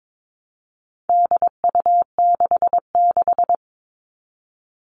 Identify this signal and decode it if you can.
Morse code, heard as a keyed tone, about 22 wpm, 700 Hz. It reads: DU66
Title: All Cut Numbers - 4-Digit 22wpm